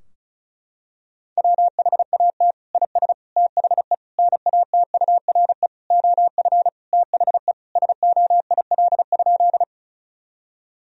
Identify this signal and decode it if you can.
Morse code, heard as a keyed tone, about 36 wpm, 700 Hz. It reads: WHAT IS THE NATURE OF THE SOIL?
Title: Sentences from Top 1000 Words - Review Part 4 - 35wpm